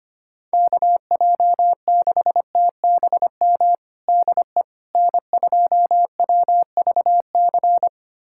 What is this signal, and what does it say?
Morse code, heard as a keyed tone, about 25 wpm, 700 Hz. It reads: KJ6TBM DE N2WVC
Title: USA Callsign DE Callsign 25wpm